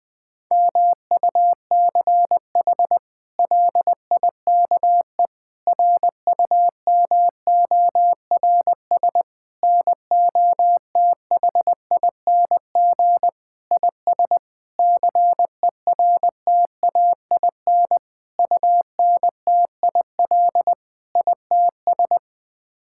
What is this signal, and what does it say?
Morse code, heard as a keyed tone, about 21 wpm, 700 Hz. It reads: MUCH LIKE RUMORS NOTHING IS CERTAIN UNTIL ITS
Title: Sets of 8 Words - 20wpm